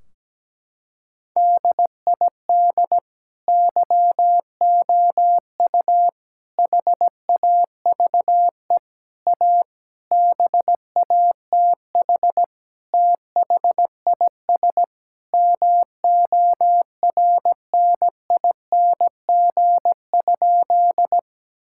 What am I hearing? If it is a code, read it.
Morse code, heard as a keyed tone, about 17 wpm, 700 Hz. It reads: DID YOU HAVE A BATH THIS MORNING?
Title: Sentences from Top 1600 Words 17wpm